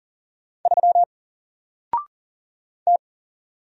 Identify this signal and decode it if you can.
Morse code, heard as a keyed tone, about 40 wpm, 700 Hz. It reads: 3 T
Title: Single Character (Letter:Number:Punctuation:Prosign) Alt 40wpm